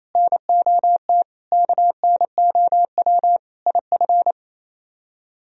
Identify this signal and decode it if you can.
Morse code, heard as a keyed tone, about 36 wpm, 700 Hz. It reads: NOT KNOW IF